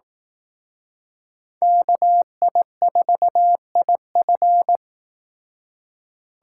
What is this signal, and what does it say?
Morse code, heard as a keyed tone, about 18 wpm, 700 Hz. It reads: KI4IF
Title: Speed Racing - USA 2x2 Call Signs 15wpm